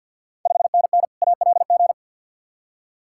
Morse code, heard as a keyed tone, about 46 wpm, 700 Hz. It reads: 5NN ALG
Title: Ontario QSO Party Exchange 50wpm